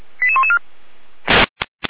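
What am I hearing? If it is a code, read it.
Five-tone selective call (ZVEI), heard as quick runs of five tones, tones of about 70 ms. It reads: 88114
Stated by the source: ZVEI five-tone selective call received with an SDR.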